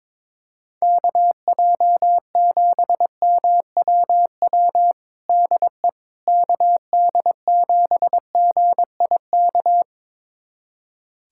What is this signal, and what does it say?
Morse code, heard as a keyed tone, about 22 wpm, 700 Hz. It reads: KJ7MWW DE KD7GIK
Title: USA Callsign DE Callsign - 6 Characters - 22wpm